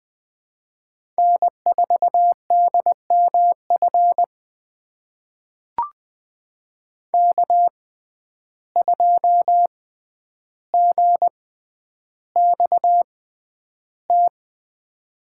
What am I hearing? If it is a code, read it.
Morse code, heard as a keyed tone, about 20 wpm, spaced as for 10 wpm, 700 Hz. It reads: N4DMF K2GXT
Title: USA Call Signs - 6x Character Spacing - 20wpm